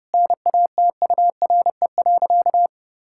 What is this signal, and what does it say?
Morse code, heard as a keyed tone, about 32 wpm, 700 Hz. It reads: NATURE.
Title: Sentences from Top 1000 Words - Review Part 6 - 30wpm